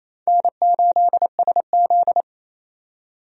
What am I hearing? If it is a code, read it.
Morse code, heard as a keyed tone, about 28 wpm, 700 Hz. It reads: N8SZ
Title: USA Call Signs - 4 Characters - 28wpm